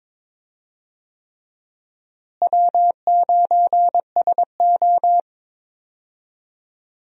Morse code, heard as a keyed tone, about 22 wpm, 700 Hz. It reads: W9SO
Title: USA Call Signs - 4 Characters - Repeated 3x - 22wpm